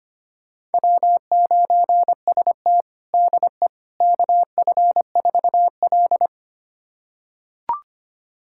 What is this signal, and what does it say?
Morse code, heard as a keyed tone, about 25 wpm, 700 Hz. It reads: W9ST DE KF4L
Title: USA Callsign DE Callsign - 4 Characters - 25wpm